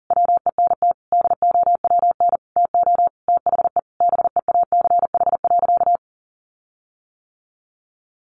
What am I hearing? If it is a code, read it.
Morse code, heard as a keyed tone, about 40 wpm, 700 Hz. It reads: WENT DOWN TO THE BEACH.
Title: Sentences from Top 1500 Words 40wpm